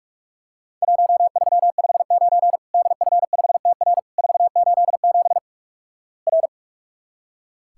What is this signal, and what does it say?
Morse code, heard as a keyed tone, about 46 wpm, 700 Hz. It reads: 1359 DF5TR 487 R
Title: Worked All Europe DX Contest - QTC Exchange - 45wpm